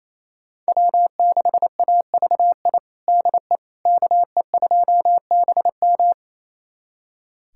Morse code, heard as a keyed tone, about 27 wpm, 700 Hz. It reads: W6AVI DE KE2BM
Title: USA Callsign DE Callsign - 5 Characters - 28wpm